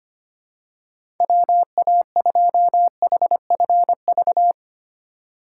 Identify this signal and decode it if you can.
Morse code, heard as a keyed tone, about 25 wpm, 700 Hz. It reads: WA2HFV
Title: USA Call Signs - 6 Characters - 6x Character Spacing - 25wpm